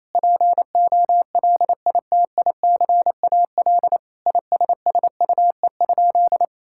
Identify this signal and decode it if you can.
Morse code, heard as a keyed tone, about 28 wpm, 700 Hz. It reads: POLITICAL ISSUE?
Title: Sentences from Top 1000 Words - Review Part 6 - 28wpm